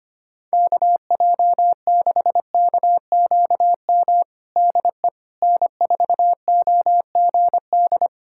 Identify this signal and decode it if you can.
Morse code, heard as a keyed tone, about 25 wpm, 700 Hz. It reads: KJ6KQM DE N4OGD
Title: Practice QSO - Exchange 4 - Ending - 25wpm